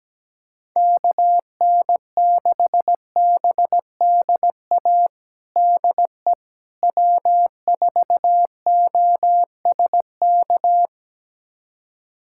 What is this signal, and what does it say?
Morse code, heard as a keyed tone, about 17 wpm, 700 Hz. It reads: KN6BDA DE W4OSK
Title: USA Callsign DE Callsign 17wpm